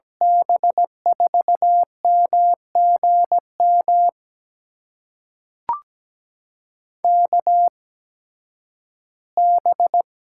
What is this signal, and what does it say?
Morse code, heard as a keyed tone, about 17 wpm, spaced as for 11 wpm, 700 Hz. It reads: B4MGM KB
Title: USA Call Signs - 6 Characters - 8x Character Spacing - 17wpm